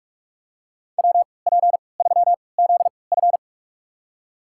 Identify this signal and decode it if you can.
Morse code, heard as a keyed tone, about 46 wpm, spaced as for 25 wpm, 700 Hz. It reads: WP3ZF
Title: USA Call Signs - 5 Characters - 3x Character Spacing - 45wpm